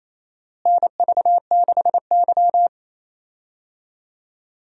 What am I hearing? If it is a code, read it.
Morse code, heard as a keyed tone, about 28 wpm, 700 Hz. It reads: NV6Y
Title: USA Call Signs - 4 Characters - Repeated 3x - 28wpm